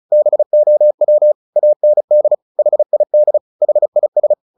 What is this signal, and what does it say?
Morse code, heard as a keyed tone, about 34 wpm, 600 Hz. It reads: BOW AND HID HIS
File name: the old man and the sea 2 35wpm